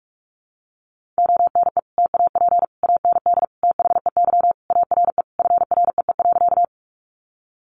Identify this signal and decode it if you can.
Morse code, heard as a keyed tone, about 44 wpm, 700 Hz. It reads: ONE TAP AND THEY ARE FREE.
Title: Sentences from Top 1400 Words 45wpm